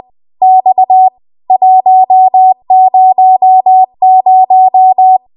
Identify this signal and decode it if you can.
Morse code, heard as a keyed tone, about 20 wpm, 750 Hz. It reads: X 100
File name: Archive / 20 WPM / 20WPM